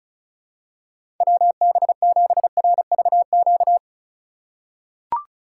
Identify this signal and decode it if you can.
Morse code, heard as a keyed tone, about 36 wpm, 700 Hz. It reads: WB7RVQ
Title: USA Call Signs - 4x Character Spacing - 35wpm